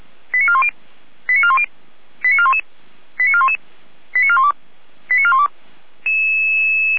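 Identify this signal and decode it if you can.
Five-tone selective call (ZVEI), heard as quick runs of five tones, tones of about 70 ms. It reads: 89410 89410 89411 89411 89412 89412
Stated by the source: ZVEI five-tone selective call 89412 received via SDR.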